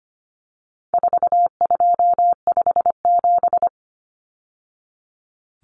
Morse code, heard as a keyed tone, about 25 wpm, 700 Hz. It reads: 4257
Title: Numbers - 4-Digit 25wpm